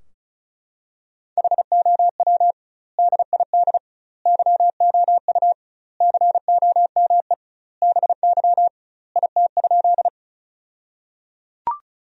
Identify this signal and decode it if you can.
Morse code, heard as a keyed tone, about 35 wpm, 700 Hz. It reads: HOW DID YOU COME BY IT?
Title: 2x Word-Spacing - Sentences from Top 100 Words - 35wpm